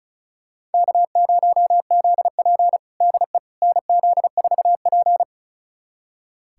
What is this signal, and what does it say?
Morse code, heard as a keyed tone, about 36 wpm, 700 Hz. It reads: K0ZP DE NZ4P
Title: USA Callsign DE Callsign - 4 Characters - 35wpm